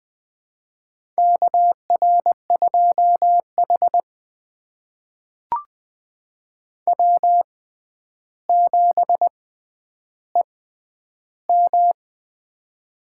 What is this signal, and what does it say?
Morse code, heard as a keyed tone, about 20 wpm, spaced as for 10 wpm, 700 Hz. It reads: KR2H W7EM
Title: USA Call Signs - 4 Characters - 6x Character Spacing - 20wpm